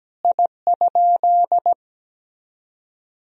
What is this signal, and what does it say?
Morse code, heard as a keyed tone, about 17 wpm, 700 Hz. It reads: I?
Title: Sentences from Top 1000 Words - Review Part 1 - 17wpm